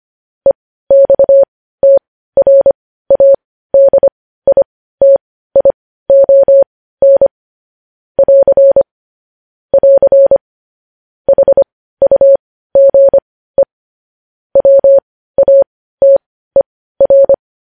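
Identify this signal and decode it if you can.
Morse code, heard as a keyed tone, about 25 wpm, spaced as for 15 wpm, 550 Hz. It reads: EXTRADITION <AR> <AR> HUGE WATER
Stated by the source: Morse code news headlines sent at 15.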